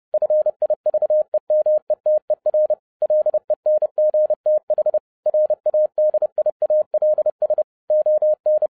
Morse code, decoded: FIVEMETER LENGTH RADIALS ON

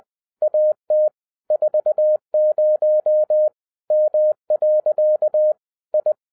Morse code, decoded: AT 40 M. I

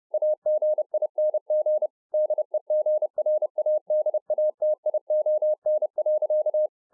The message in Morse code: AGING DEGRADATION.

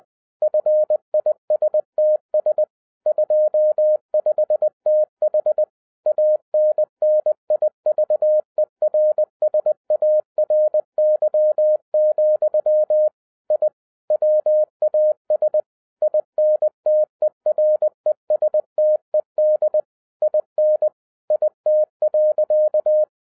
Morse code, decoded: FISTS 25TH ANNIVERSARY, I WAS INTERESTED IN IT.

